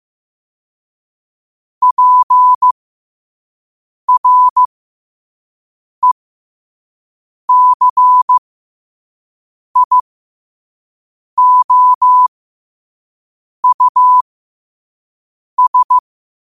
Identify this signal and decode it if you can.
Morse code, heard as a keyed tone, about 15 wpm, spaced as for 5 wpm, 1000 Hz. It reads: PRECIOUS
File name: qotdfiles / 5wpm